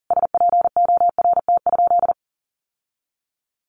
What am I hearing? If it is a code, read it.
Morse code, heard as a keyed tone, about 40 wpm, 700 Hz. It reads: SPORT?
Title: Sentences from Top 1500 Words 40wpm